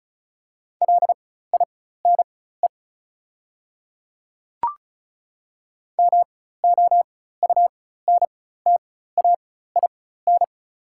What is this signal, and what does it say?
Morse code, heard as a keyed tone, about 34 wpm, spaced as for 16 wpm, 690 Hz. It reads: LINE MOUNTAIN
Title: 4x Character Spacing - Top 500 Words - 35wpm